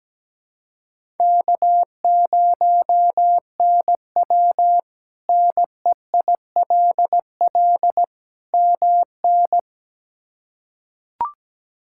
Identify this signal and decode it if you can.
Morse code, heard as a keyed tone, about 17 wpm, 700 Hz. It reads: K0NW NEILL MN
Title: K1USN SST QSO Practice - 17wpm